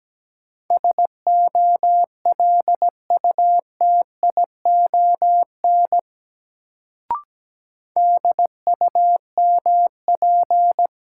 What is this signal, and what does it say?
Morse code, heard as a keyed tone, about 17 wpm, 700 Hz. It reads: SOLUTION DUMP
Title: Sentences from Top 1300 Words - New Word Drill Alt - 17wpm